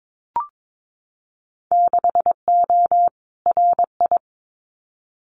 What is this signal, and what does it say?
Morse code, heard as a keyed tone, about 22 wpm, 700 Hz. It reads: 6O RI
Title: Winter Field Day Exchanges Encore 22wpm